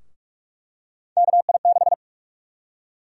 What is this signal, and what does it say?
Morse code, heard as a keyed tone, about 47 wpm, 700 Hz. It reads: KI6
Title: USA Call Sign Prefixes - 45wpm